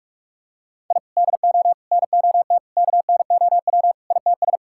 Morse code, heard as a keyed tone, about 44 wpm, 700 Hz. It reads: I DO NOT KNOW ITS
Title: Sentences from Top 1000 Words - Review Part 6 - 45wpm